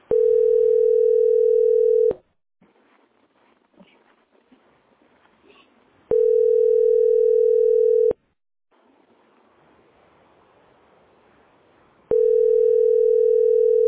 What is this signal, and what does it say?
Ringback tone (North America), 440 + 480 Hz.